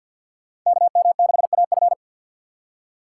Morse code, heard as a keyed tone, about 53 wpm, 700 Hz. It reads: KM6AF
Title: USA Call Signs - 5 Characters - 2x Character Spacing - 50wpm